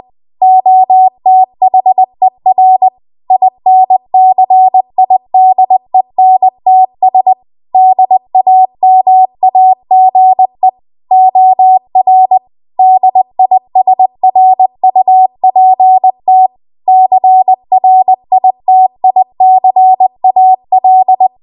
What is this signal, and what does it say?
Morse code, heard as a keyed tone, about 20 wpm, 750 Hz. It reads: OTHER INCIDENTS DAMAGE OR DISRUPT CRITICAL
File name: Morse / 20WPM